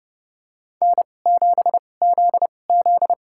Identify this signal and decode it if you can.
Morse code, heard as a keyed tone, about 30 wpm, spaced as for 22 wpm, 700 Hz. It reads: N7ZZ
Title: USA Call Signs - 4 Characters - 2x Character Spacing - 30wpm